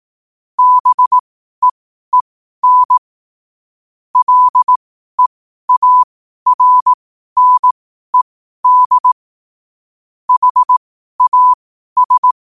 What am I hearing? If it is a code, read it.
Morse code, heard as a keyed tone, about 18 wpm, spaced as for 12 wpm, 1000 Hz. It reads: BEEN LEARNED HAS
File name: qotdfiles / 12wpm